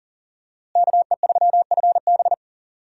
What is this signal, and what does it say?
Morse code, heard as a keyed tone, about 39 wpm, 690 Hz. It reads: KE3FB
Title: USA Call Signs - 5 Characters - Repeated 3x - 40wpm